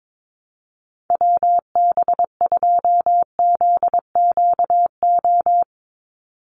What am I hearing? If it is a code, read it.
Morse code, heard as a keyed tone, about 22 wpm, 700 Hz. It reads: WB2ZQO
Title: USA Call Signs 22wpm